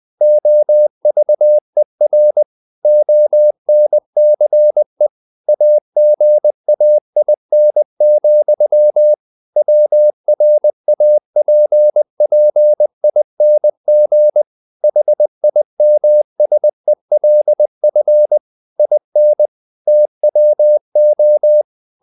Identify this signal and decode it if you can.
Morse code, heard as a keyed tone, about 20 wpm, 600 Hz. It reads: OVER ONCE AGAIN, WRAPPING HIMSELF IN TWO
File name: the old man and the sea part 2 20wpm